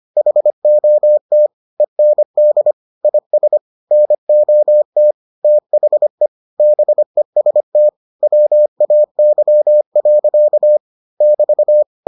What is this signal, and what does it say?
Morse code, heard as a keyed tone, about 25 wpm, 600 Hz. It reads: HOT END IS NOT THE BEST WAY. <BT>